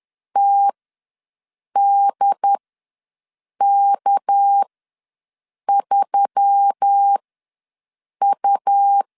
Morse code, heard as a keyed tone, about 10 wpm, spaced as for 6 wpm, 800 Hz. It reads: TDK3U
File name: LEZIONI / 6 PKX0 / 6 PK ws 10wpm 3745